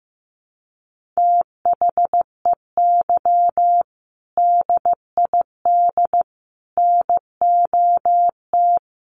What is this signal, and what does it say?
Morse code, heard as a keyed tone, about 15 wpm, 700 Hz. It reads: THEY DID NOT
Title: Sentences from Top 1500 Words 15wpm